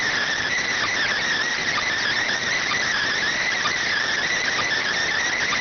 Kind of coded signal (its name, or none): Emergency Alert System (SAME) tones